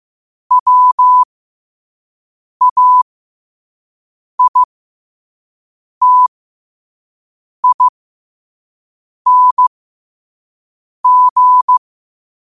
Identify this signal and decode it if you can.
Morse code, heard as a keyed tone, about 15 wpm, spaced as for 5 wpm, 1000 Hz. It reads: WAITING